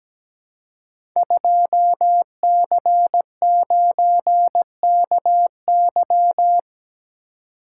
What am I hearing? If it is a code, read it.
Morse code, heard as a keyed tone, about 17 wpm, 700 Hz. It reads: 2C9KY
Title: Code Groups - 5 Letter:Numbers - 17wpm